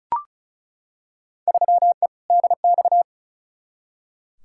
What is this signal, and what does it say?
Morse code, heard as a keyed tone, about 36 wpm, 700 Hz. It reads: 3E DX
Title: Field Day Exchanges Encore - 35wpm